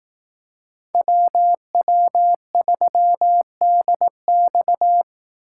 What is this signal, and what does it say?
Morse code, heard as a keyed tone, about 18 wpm, 700 Hz. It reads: WW3DX
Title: Speed Racing - USA 2x2 Call Signs 15wpm